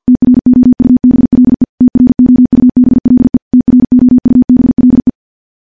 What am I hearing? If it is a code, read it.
Morse code, heard as a keyed tone, about 51 wpm, 250 Hz. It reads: TROUBLE TROUBLE TROUBLE
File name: morse / morse-training-50WPM